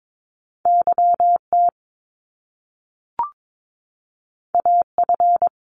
Morse code, heard as a keyed tone, about 22 wpm, 700 Hz. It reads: YT AF
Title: Top Two Letter Combinations in Words 22wpm